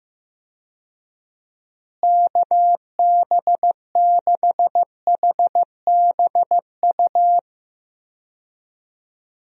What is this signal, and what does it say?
Morse code, heard as a keyed tone, about 15 wpm, 700 Hz. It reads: KB6HBU
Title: USA Call Signs - 6 Characters - Repeated 2x - 15wpm